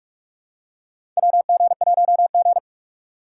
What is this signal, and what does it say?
Morse code, heard as a keyed tone, about 44 wpm, 700 Hz. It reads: WG1G
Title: USA Call Signs - 2x Character Spacing - 45wpm